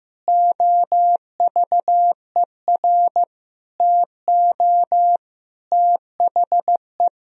Morse code, heard as a keyed tone, about 16 wpm, 700 Hz. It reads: OVER TO THE
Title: Sentences from Top 1000 Words - Review Part 2 - 15wpm